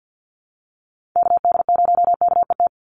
Morse code, heard as a keyed tone, about 50 wpm, 700 Hz. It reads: KD0KET
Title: USA Call Signs 50wpm